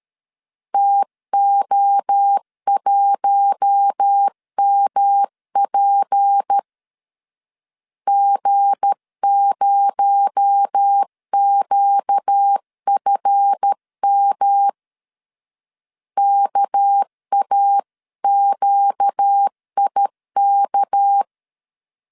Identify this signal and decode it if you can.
Morse code, heard as a keyed tone, about 13 wpm, 800 Hz. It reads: TO1MP G0QFM KAQIK
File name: LEZIONI / 7 GF9Z / 7 GF 12wpm 2122